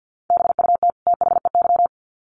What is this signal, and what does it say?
Morse code, heard as a keyed tone, about 50 wpm, 700 Hz. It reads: BUT THEY
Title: Sets of 2 Words 50wpm